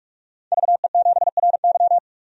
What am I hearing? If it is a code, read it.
Morse code, heard as a keyed tone, about 46 wpm, 700 Hz. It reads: VE7RY
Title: DXCC-100 Call Signs - 5 Characters - 45wpm